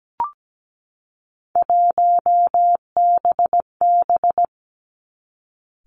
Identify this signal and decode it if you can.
Morse code, heard as a keyed tone, about 17 wpm, 700 Hz. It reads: 1BB
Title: Three Challenging Letter-Number Combinations 17wpm